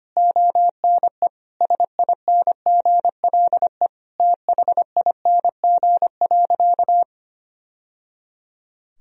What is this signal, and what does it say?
Morse code, heard as a keyed tone, about 25 wpm, 700 Hz. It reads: ONE SINGLE THING.
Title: Sentences from Top 1000 Words - Review Part 4 - 25wpm